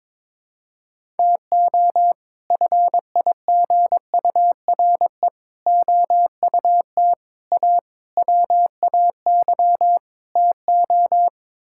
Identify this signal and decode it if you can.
Morse code, heard as a keyed tone, about 22 wpm, 700 Hz. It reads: TO FIGURE OUT A WAY TO
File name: practice / 22wpm / Sets of 6 Words - 22wpm